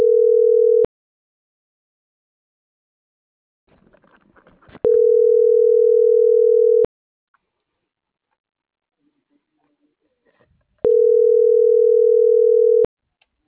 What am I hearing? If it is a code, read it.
Ringback tone (North America), 440 + 480 Hz.